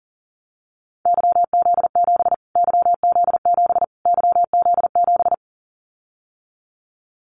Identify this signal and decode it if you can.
Morse code, heard as a keyed tone, about 40 wpm, 700 Hz. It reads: YZ7 YZ7 YZ7
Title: Callsign Prefixes for Top 25 DXCC - Repeated 3x - 40wpm